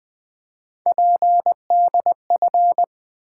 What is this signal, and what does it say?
Morse code, heard as a keyed tone, about 20 wpm, 700 Hz. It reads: PDF
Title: Easy Words with Hints - 20wpm